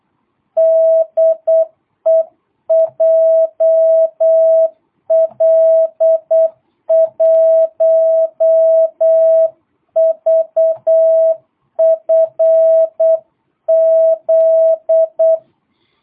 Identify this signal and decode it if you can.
Morse code, heard as a keyed tone, about 8 wpm, 660 Hz. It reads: DE JL1VFZ
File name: jh8cbh / 2022cw